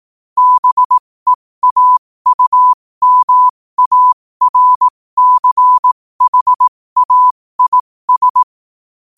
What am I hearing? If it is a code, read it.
Morse code, heard as a keyed tone, about 18 wpm, spaced as for 15 wpm, 1000 Hz. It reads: BEAUMARCHAIS
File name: qotdfiles / 15wpm